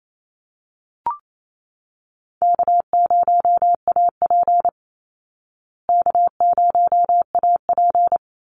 Morse code, heard as a keyed tone, about 28 wpm, 700 Hz. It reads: K0AP K0AP
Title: Easy Call Signs Repeated 2x - 28wpm